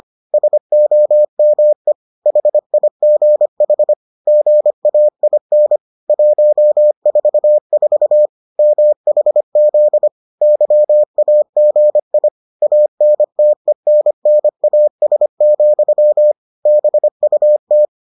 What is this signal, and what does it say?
Morse code, heard as a keyed tone, about 25 wpm, 600 Hz. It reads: SOME HIGH GAIN 144 MHZ YAGI ANTENNAS, BUT